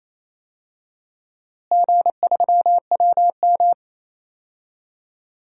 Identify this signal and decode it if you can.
Morse code, heard as a keyed tone, about 28 wpm, 700 Hz. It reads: G3WM
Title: DXCC-100 Call Signs - 4 Characters - Repeated 2x - 28wpm